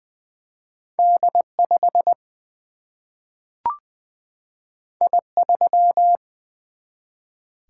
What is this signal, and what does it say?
Morse code, heard as a keyed tone, about 21 wpm, 700 Hz. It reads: D 5 I 3
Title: Single-Letter Single-Number 20wpm